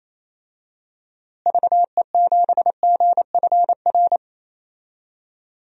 Morse code, heard as a keyed tone, about 27 wpm, 700 Hz. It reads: VE7GFR
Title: DXCC-100 Call Signs - 6 Characters - Repeated 3x - 28wpm